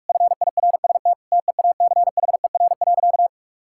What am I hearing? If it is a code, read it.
Morse code, heard as a keyed tone, about 46 wpm, 700 Hz. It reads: FIRST TEACHER.
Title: Sentences from Top 1000 Words - Review Part 6 - 45wpm